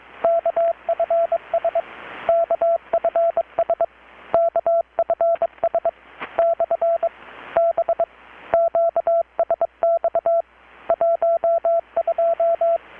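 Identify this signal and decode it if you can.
Morse code, heard as a keyed tone, about 23 wpm, 670 Hz, with radio noise. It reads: KFS KFS KFS / B QSX 12
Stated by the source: Morse from KPH's Night of Nights XX maritime radiotelegraphy event, received in Chiapas, Mexico about 3,800 km away on a Kenwood R-600 with a grounded 10 m long-wire antenna.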